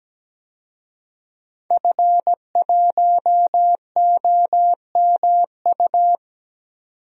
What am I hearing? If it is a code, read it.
Morse code, heard as a keyed tone, about 17 wpm, 700 Hz. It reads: F1OMU
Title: DXCC-100 Call Signs - 5 Characters - 17wpm